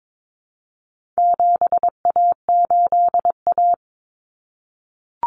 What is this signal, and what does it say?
Morse code, heard as a keyed tone, about 22 wpm, 700 Hz. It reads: 7A8A
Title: Common Cut Numbers - 4-Digit 22wpm